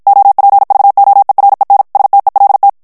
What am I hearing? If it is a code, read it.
Morse code, heard as a keyed tone, about 56 wpm, 800 Hz. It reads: OPVOEREN STELT
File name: HOLLANDAIS texte clair 2 vitesse 50 mots minute